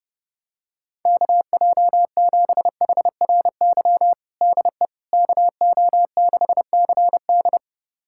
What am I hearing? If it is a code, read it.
Morse code, heard as a keyed tone, about 32 wpm, 700 Hz. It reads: KJ7HRY DE KO6CD
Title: USA Callsign DE Callsign 30wpm